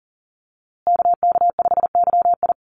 Morse code, heard as a keyed tone, about 39 wpm, 700 Hz. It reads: KK5YI
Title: USA Call Signs Repeated 2x 40wpm